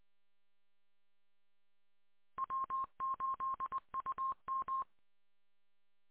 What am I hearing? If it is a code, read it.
Morse code, heard as a keyed tone, about 20 wpm, 1060 Hz. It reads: W8UM